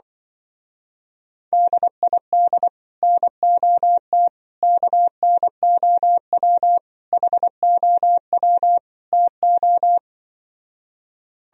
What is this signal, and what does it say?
Morse code, heard as a keyed tone, about 25 wpm, 700 Hz. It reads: DID NOT KNOW HOW TO